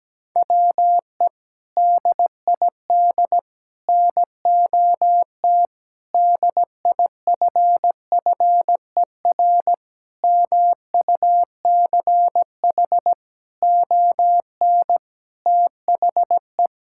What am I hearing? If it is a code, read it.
Morse code, heard as a keyed tone, about 17 wpm, 700 Hz. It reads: WE DID NOT DIFFER MUCH ON THE
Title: Sentences from Top 1000 Words - Review Part 6 - 17wpm